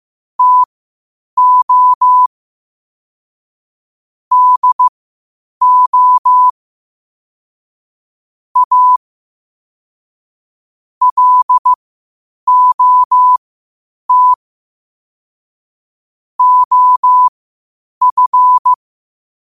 Morse code, heard as a keyed tone, about 15 wpm, spaced as for 8 wpm, 1000 Hz. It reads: TO DO A LOT OF
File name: qotdfiles / 8wpm